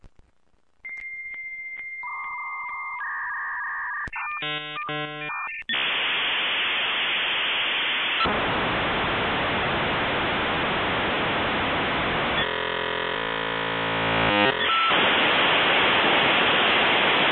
Dial-up modem handshake.